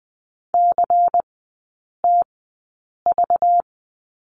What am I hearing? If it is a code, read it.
Morse code, heard as a keyed tone, about 20 wpm, spaced as for 8 wpm, 700 Hz. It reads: CTV
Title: 191 Lesson 31e - Letter v - Three Characters 20wpm